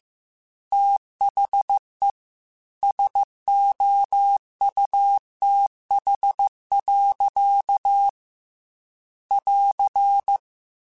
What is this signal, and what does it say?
Morse code, heard as a keyed tone, about 15 wpm, 800 Hz. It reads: THE SOUTH. <AR>